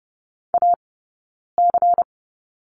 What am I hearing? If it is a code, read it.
Morse code, heard as a keyed tone, about 30 wpm, spaced as for 9 wpm, 690 Hz. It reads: AC
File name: Lesson 18d - Letter m - Two Characters 30wpm